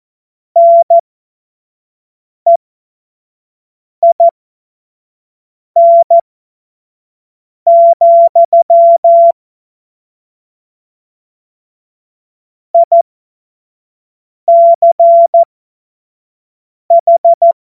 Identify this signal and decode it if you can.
Morse code, heard as a keyed tone, about 14 wpm, spaced as for 5 wpm, 680 Hz. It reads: NEIN, ICH
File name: Morsekurs / Witze / Witz 01. Februar 2022 25 F70